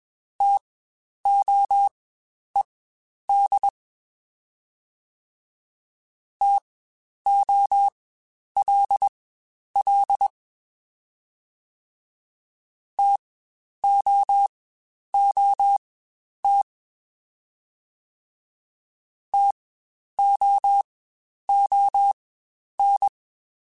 Morse code, heard as a keyed tone, about 21 wpm, spaced as for 10 wpm, 800 Hz. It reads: TOED TOLL TOOT TOON
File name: Spaced Alphas / Spaced Alphas mp3s / Ts sp @20